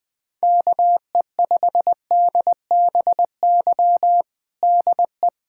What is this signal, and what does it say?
Morse code, heard as a keyed tone, about 20 wpm, 700 Hz. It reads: KE5DBY DE
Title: USA Callsign DE Callsign - 6 Characters - 20wpm